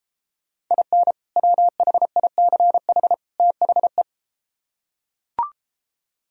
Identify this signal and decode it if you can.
Morse code, heard as a keyed tone, about 33 wpm, 700 Hz. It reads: IN WHICH THE